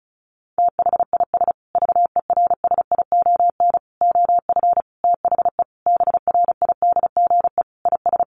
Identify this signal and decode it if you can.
Morse code, heard as a keyed tone, about 34 wpm, 700 Hz. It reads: THIS VERSION OF THE BRIDGE IS
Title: Sentences from Top 1400 Words 35wpm